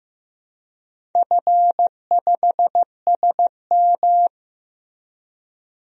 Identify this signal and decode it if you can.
Morse code, heard as a keyed tone, about 15 wpm, 700 Hz. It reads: F5SM
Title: DXCC-100 Call Signs - 4 Characters - Repeated 3x - 15wpm